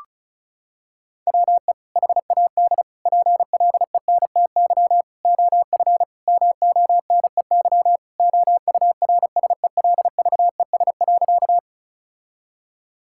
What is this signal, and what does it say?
Morse code, heard as a keyed tone, about 36 wpm, 700 Hz. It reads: WE HAD PLENTY OF MONEY OURSELVES.
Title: Sentences from Top 1600 Words 35wpm